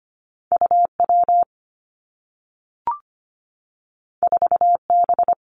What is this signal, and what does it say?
Morse code, heard as a keyed tone, about 25 wpm, 700 Hz. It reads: UW 4B
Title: Two Challenging Letter-Number Combinations 25wpm